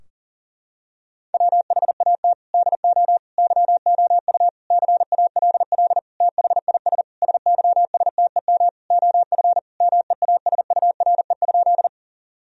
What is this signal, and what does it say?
Morse code, heard as a keyed tone, about 37 wpm, 700 Hz. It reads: WHAT DO YOU CALL THIS SYSTEM OF MEASURE?
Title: Sentences from Top 1000 Words - Review Part 3 - 40wpm